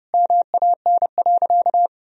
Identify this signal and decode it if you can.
Morse code, heard as a keyed tone, about 31 wpm, 700 Hz. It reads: MAN.
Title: Sentences from Top 400 Words and Abbreviations 30wpm